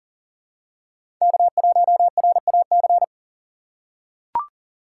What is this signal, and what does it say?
Morse code, heard as a keyed tone, about 40 wpm, 700 Hz. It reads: K1RAC
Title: USA Call Signs - 5 Characters - 4x Character Spacing - 40wpm